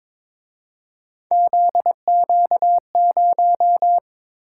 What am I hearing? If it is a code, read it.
Morse code, heard as a keyed tone, about 26 wpm, 700 Hz. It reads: ZQ0
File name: Speed-Racing - Three Letter-Number Combinations - 22wpm